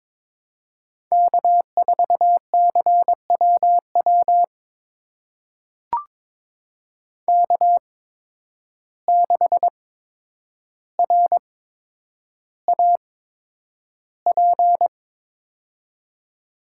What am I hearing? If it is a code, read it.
Morse code, heard as a keyed tone, about 22 wpm, spaced as for 9 wpm, 700 Hz. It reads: K4CWW K6RAP
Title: USA Call Signs - 5 Characters - 8x Character Spacing - 22wpm